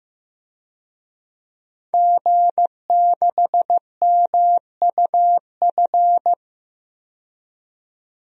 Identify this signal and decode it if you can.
Morse code, heard as a keyed tone, about 15 wpm, 700 Hz. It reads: G6MUF
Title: DXCC-100 Call Signs - 5 Characters - 15wpm